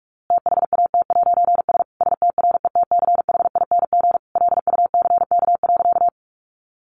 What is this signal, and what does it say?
Morse code, heard as a keyed tone, about 44 wpm, 700 Hz. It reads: THAT'S STRETCHING LUCK.
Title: A Sound Of Thunder 45wpm